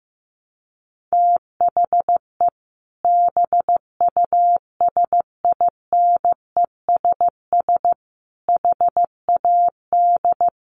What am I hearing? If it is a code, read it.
Morse code, heard as a keyed tone, about 15 wpm, 700 Hz. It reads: THE BUSINESS HAD
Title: Sentences from Top 1400 Words 15wpm